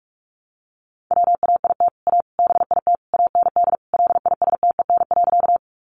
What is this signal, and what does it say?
Morse code, heard as a keyed tone, about 44 wpm, 700 Hz. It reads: WAIT A BIT AND LISTEN.
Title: Sentences from Top 900 Words 45wpm